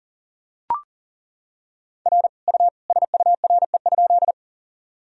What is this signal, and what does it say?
Morse code, heard as a keyed tone, about 40 wpm, 700 Hz. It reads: R U SURE?
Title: Sentences from Top 300 Words and Abbreviations 40wpm